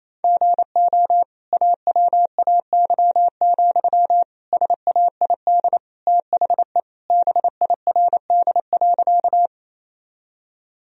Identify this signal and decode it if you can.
Morse code, heard as a keyed tone, about 28 wpm, 700 Hz. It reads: GO AWAY, SAID THE BIRD.
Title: Sentences from Top 1000 Words - Review Part 2 - 28wpm